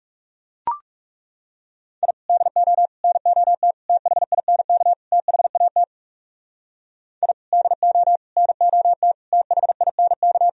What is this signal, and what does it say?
Morse code, heard as a keyed tone, about 37 wpm, 700 Hz. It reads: I DO NOT THINK THAT I DO NOT THINK
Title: Speed-Racing - Sets of 5 Words 30wpm